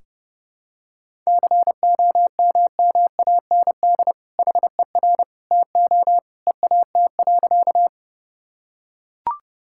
Morse code, heard as a keyed tone, about 32 wpm, 700 Hz. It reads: COMMAND HER TO EAT.
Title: Sentences from Top 1600 Words 30wpm